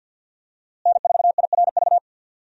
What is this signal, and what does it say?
Morse code, heard as a keyed tone, about 52 wpm, 700 Hz. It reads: N4IRV
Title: USA Call Signs - 5 Characters - Repeated 2x - 50wpm